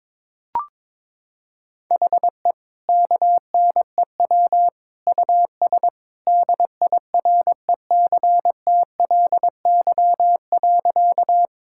Morse code, heard as a keyed tone, about 22 wpm, 700 Hz. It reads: HE KNEW US DIRECTLY.
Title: Sentences from Top 1000 Words - Review Part 5 - 22wpm